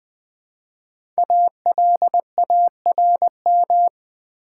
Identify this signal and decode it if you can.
Morse code, heard as a keyed tone, about 20 wpm, 690 Hz. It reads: ALARM
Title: 5 Letter Words - First 3 Letters Given - 20wpm